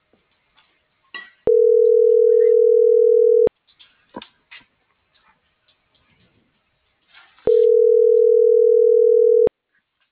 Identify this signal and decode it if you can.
Ringback tone (North America), 440 + 480 Hz.